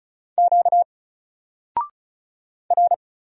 Morse code, heard as a keyed tone, about 37 wpm, 700 Hz. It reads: Q R
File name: Single-Letter-Number-35wpm